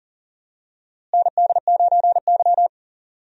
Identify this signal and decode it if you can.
Morse code, heard as a keyed tone, about 40 wpm, 700 Hz. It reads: ND9Y
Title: USA Call Signs - 4 Characters - Repeated 3x - 40wpm